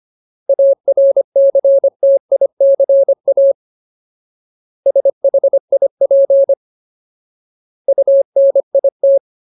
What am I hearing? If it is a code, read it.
Morse code, heard as a keyed tone, about 25 wpm, 550 Hz. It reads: ARCTICA SHIP UNIT
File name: Ultimate Word List / 25WPM Ultimate